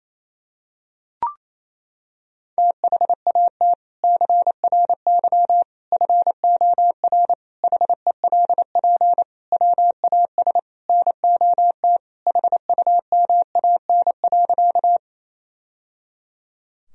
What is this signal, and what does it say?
Morse code, heard as a keyed tone, about 28 wpm, 700 Hz. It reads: THAT CRY FOR HELP WAS NOT HUMAN.
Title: Sentences from Top 1000 Words - Review Part 3 - 28wpm